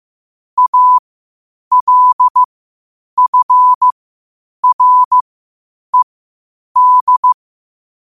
Morse code, heard as a keyed tone, about 15 wpm, spaced as for 8 wpm, 1000 Hz. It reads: ALFRED
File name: qotdfiles / 8wpm